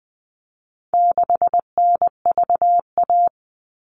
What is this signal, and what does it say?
Morse code, heard as a keyed tone, about 20 wpm, 700 Hz. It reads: 6NVA
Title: All Cut Numbers - 4-Digit 20wpm